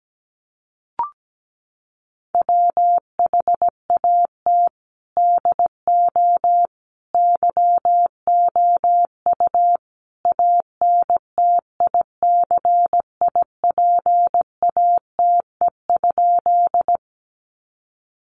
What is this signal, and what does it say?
Morse code, heard as a keyed tone, about 17 wpm, 700 Hz. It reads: WHAT DO YOU ANTICIPATE?
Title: Sentences from Top 1500 Words 17wpm